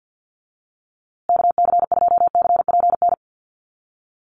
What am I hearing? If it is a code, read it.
Morse code, heard as a keyed tone, about 48 wpm, 700 Hz. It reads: KC2CPN
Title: USA Call Signs Repeated 3x 50wpm